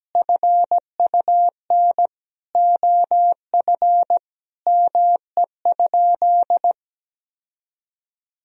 Morse code, heard as a keyed tone, about 17 wpm, 700 Hz. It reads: FUN OF ME?
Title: Sentences from Top 1000 Words - Review Part 4 - 17wpm